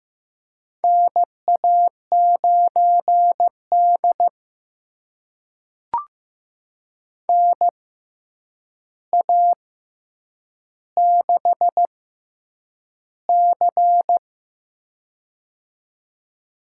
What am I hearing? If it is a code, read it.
Morse code, heard as a keyed tone, about 15 wpm, spaced as for 8 wpm, 700 Hz. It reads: NA9D NA6C
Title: USA Call Signs - 4 Characters - 6x Character Spacing - 15wpm